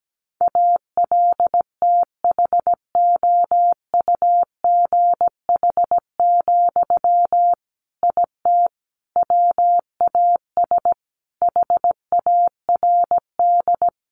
Morse code, heard as a keyed tone, about 17 wpm, 700 Hz. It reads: ALTHOUGH, IT WAS HARD
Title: Sentences from Top 600 Words 17wpm